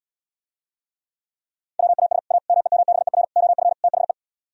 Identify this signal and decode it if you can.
Morse code, heard as a keyed tone, about 78 wpm, 700 Hz. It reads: WAS A MEMBER OF THE